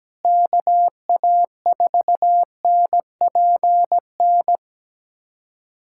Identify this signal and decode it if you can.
Morse code, heard as a keyed tone, about 17 wpm, 700 Hz. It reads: KA4NPN
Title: USA Callsign DE Callsign - 6 Characters - 17wpm